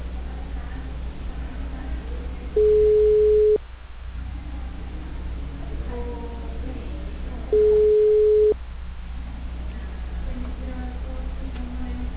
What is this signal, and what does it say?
Ringback tone (Europe, 425 Hz), with speech.
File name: out- -2016- - - .668